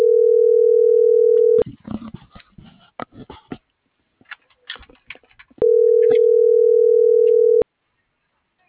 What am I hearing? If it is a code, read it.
Ringback tone (North America), 440 + 480 Hz.